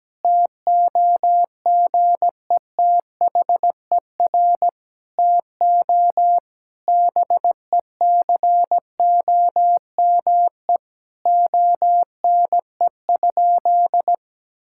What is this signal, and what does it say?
Morse code, heard as a keyed tone, about 17 wpm, 700 Hz. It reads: TOGETHER TO BECOME ONE?
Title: Sentences from Top 1600 Words 17wpm